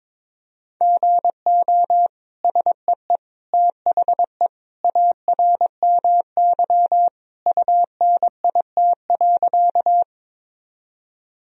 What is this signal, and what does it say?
Morse code, heard as a keyed tone, about 22 wpm, 700 Hz. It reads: GO SEE THE ARMY UNIT.
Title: Sentences from Top 1000 Words - Review Part 2 - 22wpm